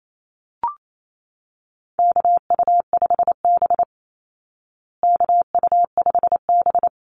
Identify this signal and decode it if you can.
Morse code, heard as a keyed tone, about 28 wpm, 700 Hz. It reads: KU5B KU5B
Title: Call Signs Repeated 2x - 28wpm